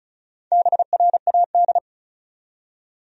Morse code, heard as a keyed tone, about 35 wpm, 700 Hz. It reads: BRAD
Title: USA Common Names - Max 4 Letters - 35wpm